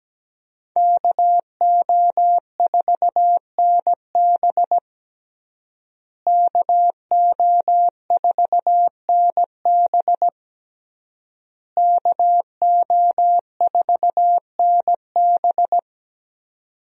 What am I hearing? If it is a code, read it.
Morse code, heard as a keyed tone, about 17 wpm, 700 Hz. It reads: KO4NB KO4NB KO4NB
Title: USA Call Signs - 5 Characters - Repeated 3x - 17wpm